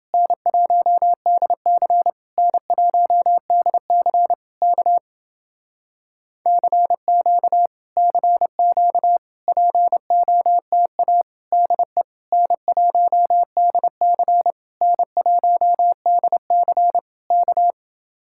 Morse code, heard as a keyed tone, about 29 wpm, 700 Hz. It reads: N1DC N1DC K CQ CQ POTA DE N1DC N1DC K